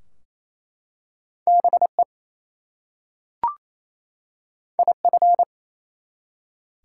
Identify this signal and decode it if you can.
Morse code, heard as a keyed tone, about 27 wpm, 700 Hz. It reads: BE IF
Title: Two Letter Words - 28wpm